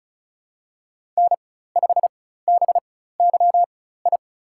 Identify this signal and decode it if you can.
Morse code, heard as a keyed tone, about 34 wpm, spaced as for 16 wpm, 700 Hz. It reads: N5BYI
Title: USA Call Signs - 4x Character Spacing - 35wpm